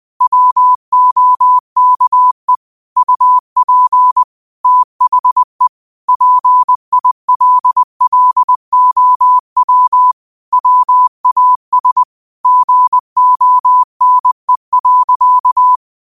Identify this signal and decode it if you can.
Morse code, heard as a keyed tone, about 20 wpm, 1000 Hz. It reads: WOKE UP THE PILLOW WAS GONE.
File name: qotdfiles / 20wpm